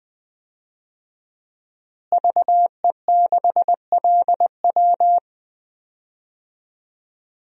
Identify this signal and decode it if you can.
Morse code, heard as a keyed tone, about 20 wpm, 700 Hz. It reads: VE6LW
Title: DXCC-100 Call Signs - 5 Characters - Repeated 3x - 20wpm